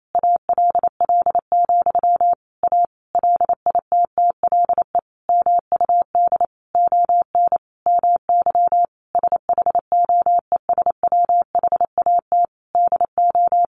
Morse code, heard as a keyed tone, about 28 wpm, 700 Hz. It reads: ALL, A LITTLE MUD ON MY SHOES WHAT DO
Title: A Sound Of Thunder 28wpm